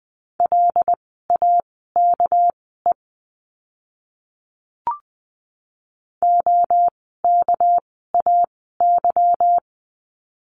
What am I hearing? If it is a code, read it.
Morse code, heard as a keyed tone, about 20 wpm, spaced as for 15 wpm, 700 Hz. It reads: LAKE OKAY
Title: 199 Lesson 32f - Letter k - Words 20wpm